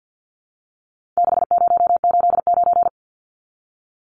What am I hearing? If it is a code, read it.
Morse code, heard as a keyed tone, about 50 wpm, 700 Hz. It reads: 6089
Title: Numbers - 4-Digit 50wpm